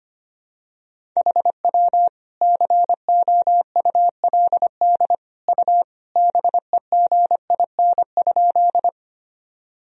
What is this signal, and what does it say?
Morse code, heard as a keyed tone, about 25 wpm, 700 Hz. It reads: HW COULD U BEGIN?
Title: Sentences from Top 300 Words and Abbreviations 25wpm